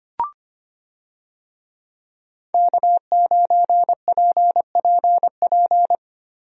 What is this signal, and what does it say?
Morse code, heard as a keyed tone, about 25 wpm, 700 Hz. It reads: K9PPP
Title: USA Call Signs - 5 Characters - Repeated 3x - 25wpm